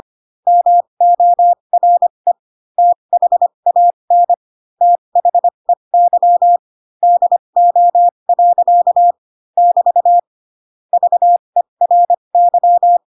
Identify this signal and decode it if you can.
Morse code, heard as a keyed tone, about 25 wpm, spaced as for 22 wpm, 700 Hz. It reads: MORE THAN THEY DO. <BT> VERY
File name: Pinocchio / Pinocchio ch4 18 25 ik1hge